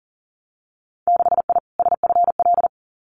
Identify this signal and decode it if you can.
Morse code, heard as a keyed tone, about 40 wpm, 700 Hz. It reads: 6I SFL
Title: Winter Field Day Exchanges 40wpm